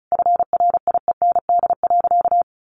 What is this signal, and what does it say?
Morse code, heard as a keyed tone, about 34 wpm, 700 Hz. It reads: FRIEND.
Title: Sentences from Top 600 Words 35wpm